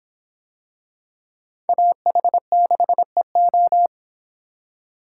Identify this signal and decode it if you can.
Morse code, heard as a keyed tone, about 24 wpm, 700 Hz. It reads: AH6EO